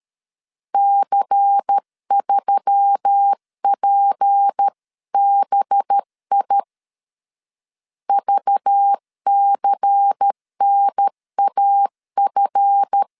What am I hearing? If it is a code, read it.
Morse code, heard as a keyed tone, about 13 wpm, 800 Hz. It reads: C3PBI VCNAF
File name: LEZIONI / 7 GF9Z / 7 GF 12wpm 2122